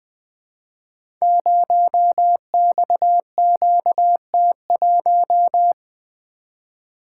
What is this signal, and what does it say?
Morse code, heard as a keyed tone, about 20 wpm, 700 Hz. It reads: 0XQT1
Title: Code Groups - 5 Letter:Numbers - 20wpm